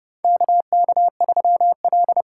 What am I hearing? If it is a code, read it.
Morse code, heard as a keyed tone, about 30 wpm, 700 Hz. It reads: KK3L
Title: USA Call Signs - 4 Characters - 3x Character Spacing - 30wpm